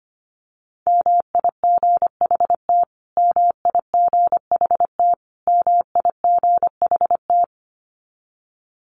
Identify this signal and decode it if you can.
Morse code, heard as a keyed tone, about 25 wpm, 700 Hz. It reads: MIGHT MIGHT MIGHT
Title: Top 500 Words Repeated 25wpm